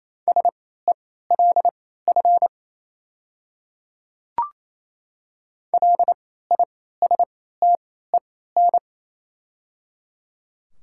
Morse code, heard as a keyed tone, about 27 wpm, spaced as for 16 wpm, 700 Hz. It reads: SELF LISTEN
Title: 3x Character Spacing - Top 500 Words - 28wpm